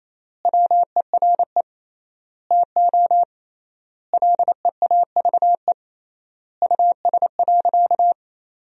Morse code, heard as a keyed tone, about 28 wpm, 700 Hz. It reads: WERE TO LEAVE US.
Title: 3x Word-Spacing - Sentences from Top 400 Words - 28wpm